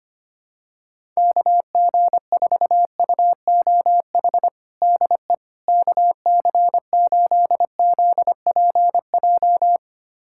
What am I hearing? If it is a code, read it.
Morse code, heard as a keyed tone, about 25 wpm, 700 Hz. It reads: KG4UOH DE KC8ZPJ
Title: USA Callsign DE Callsign - 6 Characters - 25wpm